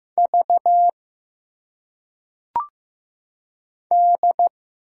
Morse code, heard as a keyed tone, about 15 wpm, 700 Hz. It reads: V D
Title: Single Letters - Phonetic Pronunciation 15wpm